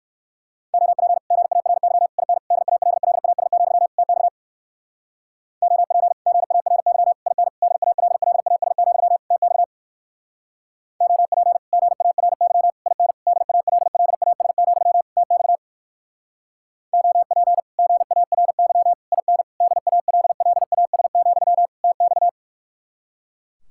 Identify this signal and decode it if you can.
Morse code, heard as a keyed tone, about 57 wpm, 700 Hz. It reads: OP GARY IN DALLAS, TX OP GARY IN DALLAS, TX OP GARY IN DALLAS, TX OP GARY IN DALLAS, TX